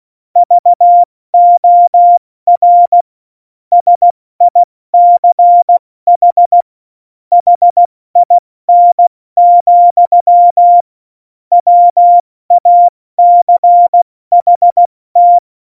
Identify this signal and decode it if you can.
Morse code, heard as a keyed tone, about 16 wpm, 700 Hz. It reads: VOR SICH HIN, WACHT
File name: Morsekurs / Witze / Witz 08. Dezember 2020, 70 F80